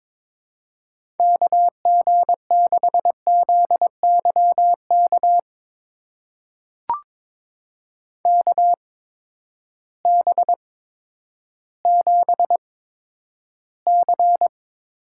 Morse code, heard as a keyed tone, about 22 wpm, spaced as for 11 wpm, 700 Hz. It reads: KG6ZYK KB7C
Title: USA Call Signs - 6 Characters - 8x Character Spacing - 22wpm